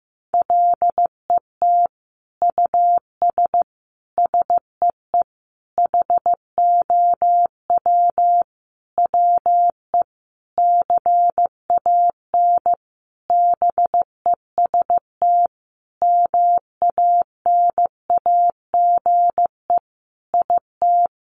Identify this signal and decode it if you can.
Morse code, heard as a keyed tone, about 15 wpm, 700 Hz. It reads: LET US SEE HOW WE CAN BEST MANAGE IT
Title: Sentences from Top 1400 Words 15wpm